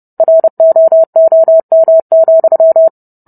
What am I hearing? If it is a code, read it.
Morse code, heard as a keyed tone, about 30 wpm, 650 Hz. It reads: ROOM,